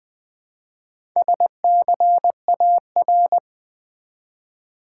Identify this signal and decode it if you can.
Morse code, heard as a keyed tone, about 20 wpm, 700 Hz. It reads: SCAR
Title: Build A Word - 1 Letter At A Time - 5-Letter Words - 20wpm